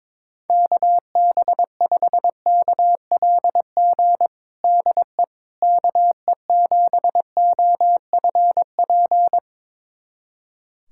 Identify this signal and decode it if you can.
Morse code, heard as a keyed tone, about 22 wpm, 700 Hz. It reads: KB5KLG DE KE7OFP
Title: USA Callsign DE Callsign 22wpm